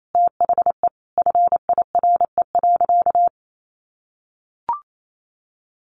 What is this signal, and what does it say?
Morse code, heard as a keyed tone, about 28 wpm, 700 Hz. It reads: THE FIRE.
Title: Sentences from Top 1400 Words 28wpm